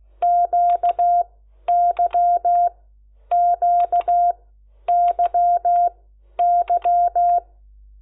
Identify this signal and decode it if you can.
Morse code, heard as a keyed tone, about 16 wpm, spaced as for 10 wpm, 700 Hz. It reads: QYQYY
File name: Lecciones